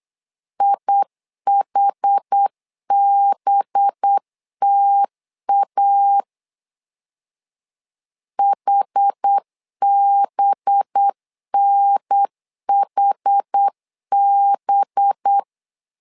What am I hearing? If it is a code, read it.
Morse code, heard as a keyed tone, about 8 wpm, 800 Hz. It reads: IHBTA HBNHB
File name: LEZIONI / 5 LBVC / 5 LB 8wpm 2816